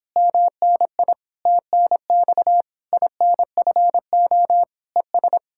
Morse code, heard as a keyed tone, about 29 wpm, 700 Hz. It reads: MNI TNX INFO ES